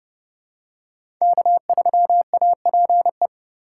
Morse code, heard as a keyed tone, about 30 wpm, 700 Hz. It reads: K3APE
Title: USA Call Signs - 3x Character Spacing - 30wpm